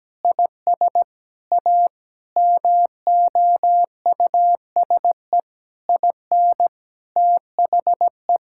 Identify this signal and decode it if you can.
Morse code, heard as a keyed tone, about 17 wpm, 700 Hz. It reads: IS A MOUSE IN THE
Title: Sentences from Top 1000 Words - Review Part 5 - 17wpm